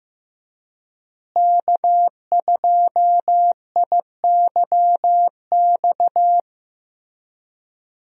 Morse code, heard as a keyed tone, about 15 wpm, 700 Hz. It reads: K2IYX
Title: USA Call Signs - 5 Characters - Repeated 2x - 15wpm